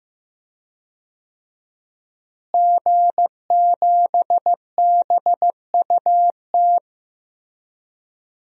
Morse code, heard as a keyed tone, about 15 wpm, 700 Hz. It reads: G7BUT
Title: DXCC-100 Call Signs - 5 Characters - Repeated 3x - 15wpm